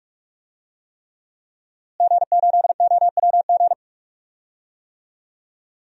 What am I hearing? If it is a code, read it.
Morse code, heard as a keyed tone, about 42 wpm, 700 Hz. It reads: G8OWG
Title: DXCC-100 Call Signs - 5 Characters - Repeated 3x - 45wpm